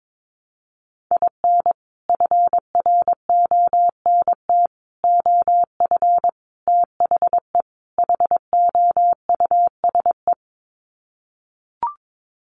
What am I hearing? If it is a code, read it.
Morse code, heard as a keyed tone, about 22 wpm, 700 Hz. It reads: IN FRONT OF THE HOUSE
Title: Sets of 5 Words Encore2 22wpm